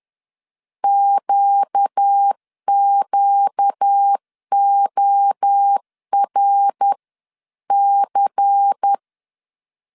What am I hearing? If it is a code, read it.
Morse code, heard as a keyed tone, about 10 wpm, spaced as for 9 wpm, 800 Hz. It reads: QQORC
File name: CW / DET / 1 6 1500C S C 10wpm 4252